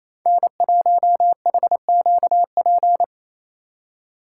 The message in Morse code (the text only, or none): N1HQP